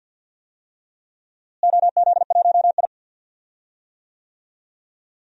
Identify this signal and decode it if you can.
Morse code, heard as a keyed tone, about 52 wpm, 700 Hz. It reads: OZ1I
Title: DXCC-100 Call Signs - 4 Characters - Repeated 2x - 50wpm